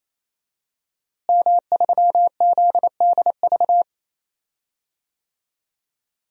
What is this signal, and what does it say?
Morse code, heard as a keyed tone, about 28 wpm, 700 Hz. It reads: M3ZDV
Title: DXCC-100 Call Signs - 5 Characters - Repeated 3x - 28wpm